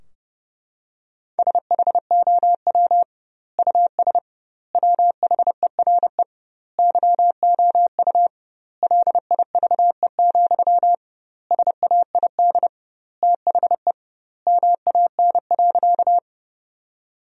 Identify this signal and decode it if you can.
Morse code, heard as a keyed tone, about 32 wpm, 700 Hz. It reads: SHOW US WHERE YOU LIVE, SAID THE MAN.
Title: 2x Word-Spacing - Sentences from Top 200 Words - 30wpm